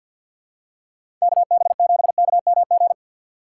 Morse code, heard as a keyed tone, about 48 wpm, 700 Hz. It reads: KB7KMG
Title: USA Call Signs - 6 Characters - 3x Character Spacing - 50wpm